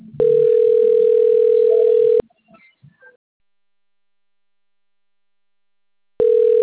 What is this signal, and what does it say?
Ringback tone (North America), 440 + 480 Hz.
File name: rg-1600-unknown- - - .3350